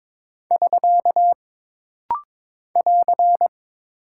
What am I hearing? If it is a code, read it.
Morse code, heard as a keyed tone, about 23 wpm, 700 Hz. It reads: <SK> <AR>
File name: practice / 22wpm / Single Punctuation-Prosign - Rapid-Fire - 22wpm